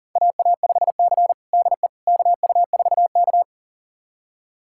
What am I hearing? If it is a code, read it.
Morse code, heard as a keyed tone, about 37 wpm, 700 Hz. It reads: AA5C DE KU4K
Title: USA Callsign DE Callsign - 4 Characters - 40wpm